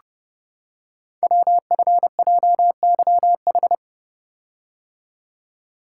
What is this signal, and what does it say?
Morse code, heard as a keyed tone, about 30 wpm, 700 Hz. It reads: WFJYH
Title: Code Groups - 5 Letters - 30wpm